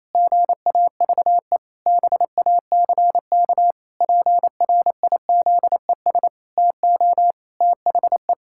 Morse code, heard as a keyed tone, about 28 wpm, 700 Hz. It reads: GAVE BACK PRIZES TO THE
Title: Pangram Challenge v2 - 28wpm